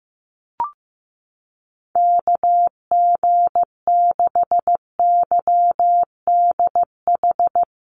Morse code, heard as a keyed tone, about 15 wpm, 700 Hz. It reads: KG6YDH
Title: USA Call Signs Repeated 3x 15wpm